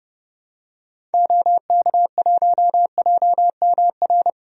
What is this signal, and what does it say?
Morse code, heard as a keyed tone, about 30 wpm, 700 Hz. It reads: OK1JMR
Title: DXCC-100 Call Signs - 6 Characters - 30wpm